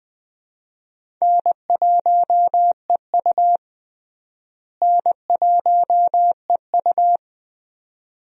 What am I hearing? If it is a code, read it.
Morse code, heard as a keyed tone, about 20 wpm, 700 Hz. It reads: N1EU N1EU
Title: USA Call Signs - 4 Characters - Repeated 2x - 20wpm